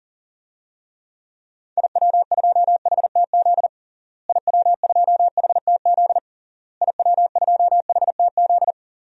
Morse code, heard as a keyed tone, about 40 wpm, 700 Hz. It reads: IW2HTZ IW2HTZ IW2HTZ
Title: DXCC-100 Call Signs - 6 Characters - Repeated 3x - 40wpm